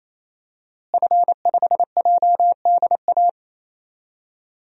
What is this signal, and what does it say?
Morse code, heard as a keyed tone, about 28 wpm, 700 Hz. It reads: F5JDA
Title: DXCC-100 Call Signs - 5 Characters - Repeated 3x - 28wpm